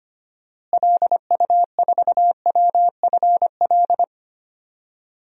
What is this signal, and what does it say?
Morse code, heard as a keyed tone, about 25 wpm, 700 Hz. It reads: LU4WFL
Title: DXCC-100 Call Signs - 6 Characters - Repeated 2x - 25wpm